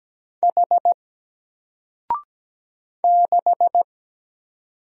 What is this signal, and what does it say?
Morse code, heard as a keyed tone, about 17 wpm, 700 Hz. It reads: H 6
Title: Single Character (Letter:Number:Punctuation:Prosign) 17wpm